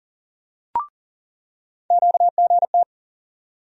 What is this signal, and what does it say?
Morse code, heard as a keyed tone, about 40 wpm, 700 Hz. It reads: QGT
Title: USA Call Sign Postfixes - 3 Letters 40wpm